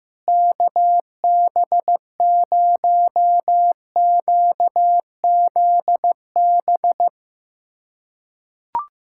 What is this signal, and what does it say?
Morse code, heard as a keyed tone, about 15 wpm, 700 Hz. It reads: KB0QZB
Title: USA Callsign DE Callsign 15wpm